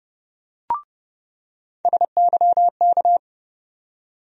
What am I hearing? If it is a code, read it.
Morse code, heard as a keyed tone, about 30 wpm, 700 Hz. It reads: SYK
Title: USA Call Sign Postfixes - 3 Letters 30wpm